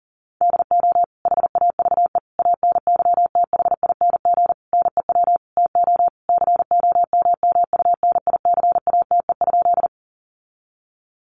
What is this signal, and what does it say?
Morse code, heard as a keyed tone, about 40 wpm, 700 Hz. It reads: DO HAVE ANYTHING NEW TO COMMUNICATE?
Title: Sentences from Top 1300 Words 40wpm